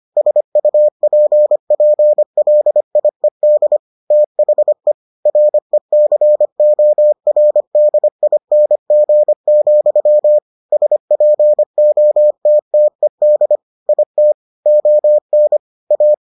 Morse code, heard as a keyed tone, about 25 wpm, 600 Hz. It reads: SUPPLIED THE RECORDING, SPOTTED IT ON A